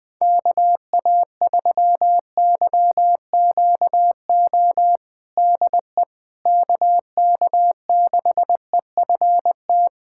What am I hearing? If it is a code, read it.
Morse code, heard as a keyed tone, about 21 wpm, 700 Hz. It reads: KA3YQO DE KK6EFT
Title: USA Callsign DE Callsign - 6 Characters - 20wpm